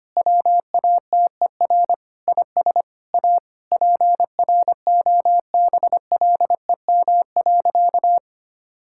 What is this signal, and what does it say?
Morse code, heard as a keyed tone, about 25 wpm, 700 Hz. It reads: WATER IS A PROBLEM.
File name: practice / 25wpm / Sentences with 3-Word Phrases - 25wpm